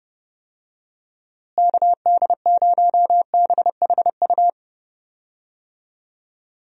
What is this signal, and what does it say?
Morse code, heard as a keyed tone, about 30 wpm, 700 Hz. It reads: KD0BHU
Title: USA Call Signs - 6 Characters - Repeated 2x - 30wpm